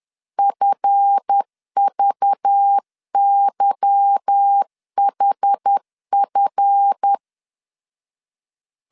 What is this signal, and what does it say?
Morse code, heard as a keyed tone, about 10 wpm, 800 Hz. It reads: FVYHF
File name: LEZIONI / 8 YJ45 / 8 Y 10wpm 2615